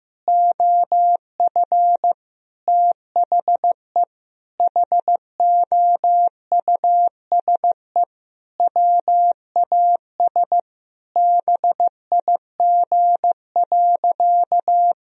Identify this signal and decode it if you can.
Morse code, heard as a keyed tone, about 15 wpm, 700 Hz. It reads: OF THE HOUSE WAS BIG.
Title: Sentences from Top 1000 Words - Review Part 2 - 15wpm